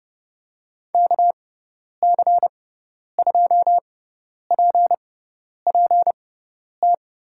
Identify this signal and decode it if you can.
Morse code, heard as a keyed tone, about 30 wpm, spaced as for 10 wpm, 700 Hz. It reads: KC2PPT
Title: USA Call Signs - 6 Characters - 6x Character Spacing - 30wpm